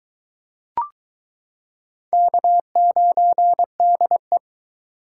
Morse code, heard as a keyed tone, about 22 wpm, 700 Hz. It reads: K9DE